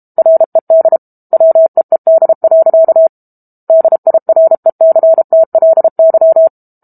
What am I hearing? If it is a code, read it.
Morse code, heard as a keyed tone, about 32 wpm, 650 Hz. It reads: RED WEED. DIRECTLY
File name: Book Two / 06-Chapter 6 - 32 WPM